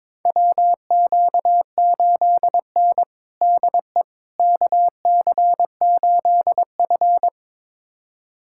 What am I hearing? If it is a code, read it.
Morse code, heard as a keyed tone, about 22 wpm, 700 Hz. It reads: WQ8N DE KC8F
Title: USA Callsign DE Callsign - 4 Characters - 22wpm